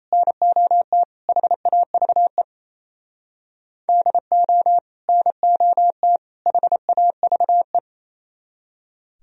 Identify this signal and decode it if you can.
Morse code, heard as a keyed tone, about 34 wpm, 700 Hz. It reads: NOT HAVE DO NOT HAVE
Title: Speed-Racing - Sets of 3 Words 28wpm